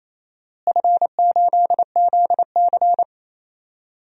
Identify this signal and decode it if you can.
Morse code, heard as a keyed tone, about 27 wpm, 700 Hz. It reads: F8ZC
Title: DXCC-100 Call Signs - 4 Characters - Repeated 3x - 28wpm